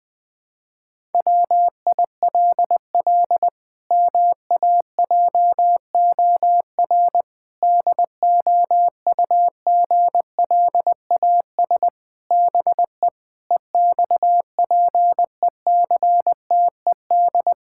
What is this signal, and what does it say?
Morse code, heard as a keyed tone, about 20 wpm, 700 Hz. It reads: WILL MAJOR DOUGLAS BE EXPECTED
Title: Pangram Challenge v2 - 20wpm